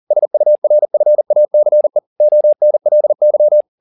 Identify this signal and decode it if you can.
Morse code, heard as a keyed tone, about 40 wpm, 600 Hz. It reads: SURFACE ONLY
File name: the old man and the sea 2 40wpm